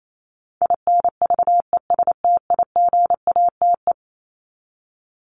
Morse code, heard as a keyed tone, about 28 wpm, 700 Hz. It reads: INVESTIGATE
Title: Top 1751-2000 Words 28wpm